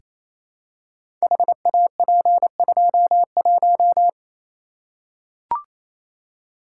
Morse code, heard as a keyed tone, about 27 wpm, 700 Hz. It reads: HAP21
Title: Code Groups - 5 Letter:Numbers - 28wpm